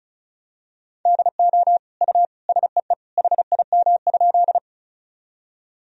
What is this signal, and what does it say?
Morse code, heard as a keyed tone, about 36 wpm, 700 Hz. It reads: DO U SEE HIM?
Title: Sentences from Top 100 Words and Abbreviations 35wpm